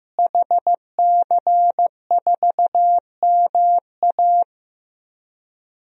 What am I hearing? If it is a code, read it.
Morse code, heard as a keyed tone, about 15 wpm, 700 Hz. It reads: HC4MA
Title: DXCC-100 Call Signs - 5 Characters - Repeated 2x - 15wpm